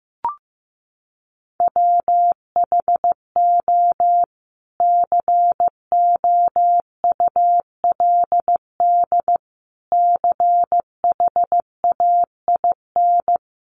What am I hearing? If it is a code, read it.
Morse code, heard as a keyed tone, about 15 wpm, 700 Hz. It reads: WHO COULD CHAIN
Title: Sentences from Top 1500 Words 15wpm